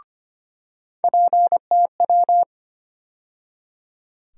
Morse code, heard as a keyed tone, about 25 wpm, 700 Hz. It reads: PTW
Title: USA Call Sign Postfixes - 3 Letters 25wpm